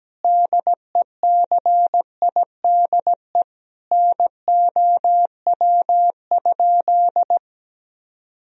Morse code, heard as a keyed tone, about 17 wpm, 700 Hz. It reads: DECIDE NOW?
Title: Sentences from Top 1000 Words - Review Part 3 - 17wpm